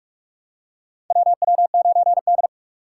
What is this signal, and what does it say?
Morse code, heard as a keyed tone, about 46 wpm, 700 Hz. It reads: WW9D
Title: USA Call Signs - 4x Character Spacing - 45wpm